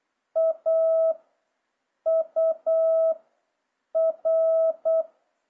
Morse code, heard as a keyed tone, about 8 wpm, 650 Hz. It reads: A U R